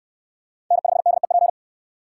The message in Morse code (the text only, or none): N5DEP